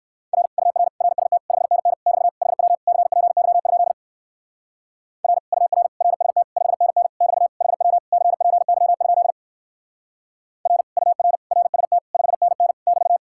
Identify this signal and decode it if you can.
Morse code, heard as a keyed tone, about 57 wpm, 700 Hz. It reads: R UR RST 5NN <BT> HW CPY? R UR RST 5NN <BT> HW CPY? R UR RST 5NN <BT>